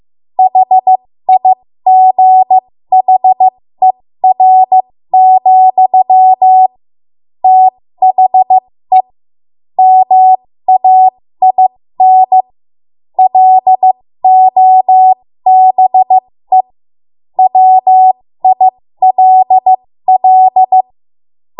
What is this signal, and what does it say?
Morse code, heard as a keyed tone, about 15 wpm, spaced as for 13 wpm, 750 Hz. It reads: HIGHER, THE MAIN LOBE WILL
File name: Morse / 13WPM